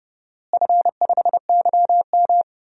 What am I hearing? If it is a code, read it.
Morse code, heard as a keyed tone, about 30 wpm, 700 Hz. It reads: F5YM
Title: DXCC-100 Call Signs - 4 Characters - Repeated 2x - 30wpm